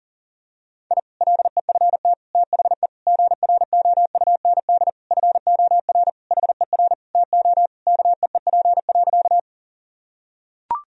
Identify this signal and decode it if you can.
Morse code, heard as a keyed tone, about 40 wpm, 700 Hz. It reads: I LEFT THE GROUND FOR HER TO KEEP.
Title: Sentences from Top 1000 Words - Review Part 2 - 40wpm